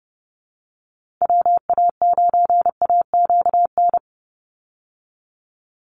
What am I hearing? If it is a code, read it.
Morse code, heard as a keyed tone, about 30 wpm, 700 Hz. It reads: WA9AQN
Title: Call Signs Repeated 2x - 30wpm